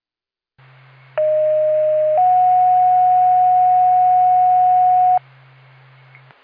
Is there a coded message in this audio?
Yes: two-tone page from 1.2 to 5.2 s, heard as a one-second tone, then a three-second tone, about 620 then 730 Hz.